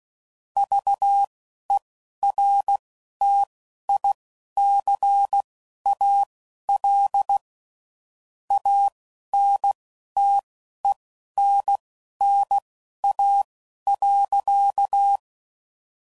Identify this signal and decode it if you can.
Morse code, heard as a keyed tone, about 16 wpm, spaced as for 11 wpm, 800 Hz. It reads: VERTICAL ANTENNA.
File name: CW Code Groups @10WPM / QSOs / QSO kc7haa